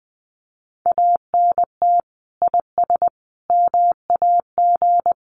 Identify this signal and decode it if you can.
Morse code, heard as a keyed tone, about 20 wpm, 700 Hz. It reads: ANT IS MAG
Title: Common QSO Phrases 20wpm